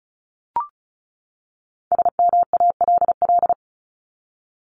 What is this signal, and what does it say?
Morse code, heard as a keyed tone, about 34 wpm, 700 Hz. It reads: SMALL
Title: Top 200 Words 35wpm